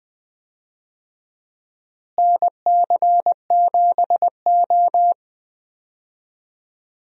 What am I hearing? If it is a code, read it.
Morse code, heard as a keyed tone, about 20 wpm, 700 Hz. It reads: NC7O
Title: USA Call Signs - 4 Characters - Repeated 3x - 20wpm